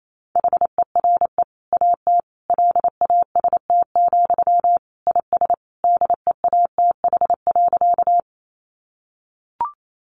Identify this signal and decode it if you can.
Morse code, heard as a keyed tone, about 28 wpm, 700 Hz. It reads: HERE AT LAST, IS DEATH.
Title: Sentences from Top 1100 Words 28wpm